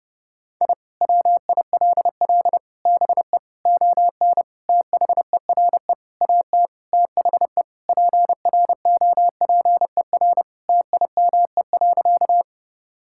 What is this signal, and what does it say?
Morse code, heard as a keyed tone, about 30 wpm, 700 Hz. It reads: I WILL BE ON THERE AT THE PROPER TIME.
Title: Sentences from Top 1000 Words - Review Part 5 - 30wpm